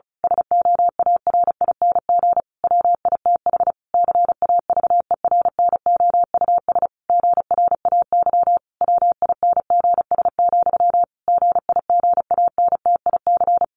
Morse code, heard as a keyed tone, about 34 wpm, 700 Hz. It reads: SOARING WITH CAVERNOUS GRAY WINGS, GIGANTIC
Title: A Sound Of Thunder 35wpm